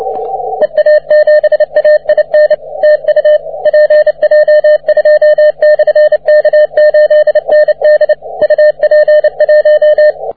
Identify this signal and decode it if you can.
Morse code, heard as a keyed tone, about 28 wpm, 600 Hz, with radio noise. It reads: EA7AIN TU PJ2/K8ND UP1
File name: wp / pj2 k8nd 40m cw